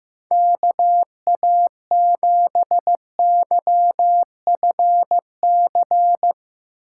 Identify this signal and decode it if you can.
Morse code, heard as a keyed tone, about 15 wpm, 700 Hz. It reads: KA7YFC
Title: Speed Racing - USA 2x3 Call Signs 15wpm